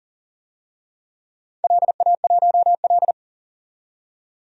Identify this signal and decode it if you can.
Morse code, heard as a keyed tone, about 40 wpm, 700 Hz. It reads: LA1L
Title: DXCC-100 Call Signs - 4 Characters - 40wpm